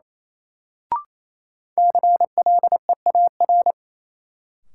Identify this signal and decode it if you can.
Morse code, heard as a keyed tone, about 28 wpm, 700 Hz. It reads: CLEAR
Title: Sets of 5 Words Encore - New Word Drill - 28wpm